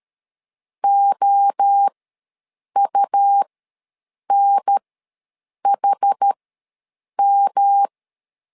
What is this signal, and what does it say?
Morse code, heard as a keyed tone, about 13 wpm, spaced as for 7 wpm, 800 Hz. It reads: OUNHM
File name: LEZIONI / 3 MWRO / 3 MWRO ws 12wpm 1914